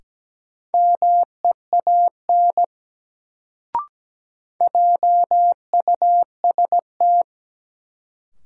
Morse code, heard as a keyed tone, about 17 wpm, 700 Hz. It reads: MEAN JUST
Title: Sets of 2 Words - New Word Drill - 17wpm